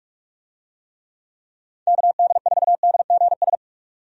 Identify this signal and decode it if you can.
Morse code, heard as a keyed tone, about 46 wpm, 700 Hz. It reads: KD4DGS
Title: USA Call Signs - 6 Characters - 45wpm